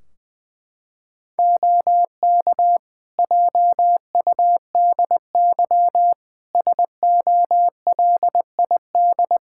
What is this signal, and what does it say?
Morse code, heard as a keyed tone, about 21 wpm, 700 Hz. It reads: OK JUDY SOLID
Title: Practice QSO - Exchange 3 - More Info - 20wpm